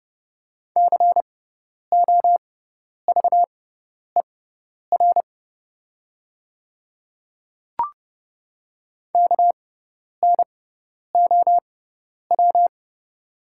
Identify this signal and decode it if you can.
Morse code, heard as a keyed tone, about 30 wpm, spaced as for 10 wpm, 700 Hz. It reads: COVER KNOW
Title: 6x Character Spacing - Top 300 Words - 30wpm